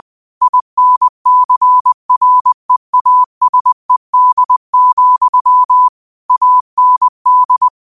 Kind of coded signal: Morse code